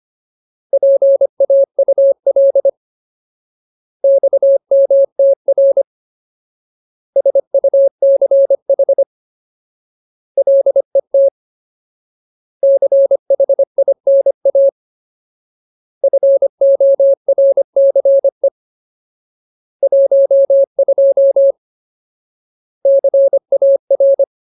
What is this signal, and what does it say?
Morse code, heard as a keyed tone, about 25 wpm, 550 Hz. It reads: PAUL XMTR SUCH LET CHINA FORCE 12 CAR
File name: Ultimate Word List / 25WPM Ultimate